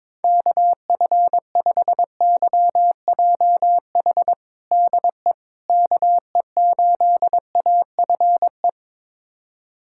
Morse code, heard as a keyed tone, about 22 wpm, 700 Hz. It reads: KF5YJH DE KE8AFE
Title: USA Callsign DE Callsign - 6 Characters - 22wpm